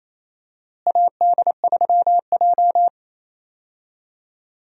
Morse code, heard as a keyed tone, about 27 wpm, 700 Hz. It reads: AD3J
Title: USA Call Signs - 4 Characters - Repeated 3x - 28wpm